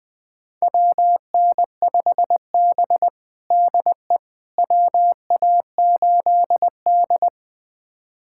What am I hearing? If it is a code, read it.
Morse code, heard as a keyed tone, about 21 wpm, 700 Hz. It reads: WN5B DE WA8D
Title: USA Callsign DE Callsign - 4 Characters - 20wpm